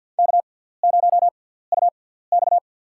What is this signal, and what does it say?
Morse code, heard as a keyed tone, about 53 wpm, spaced as for 17 wpm, 700 Hz. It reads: K0UX
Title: USA Call Signs - 4 Characters - 6x Character Spacing - 50wpm